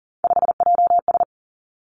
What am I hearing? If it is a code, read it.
Morse code, heard as a keyed tone, about 41 wpm, 700 Hz. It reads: 5JS
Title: Three Challenging Letter-Number Combinations 40wpm